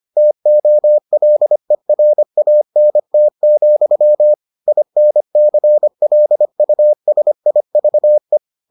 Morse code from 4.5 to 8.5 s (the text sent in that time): INCLUSIVE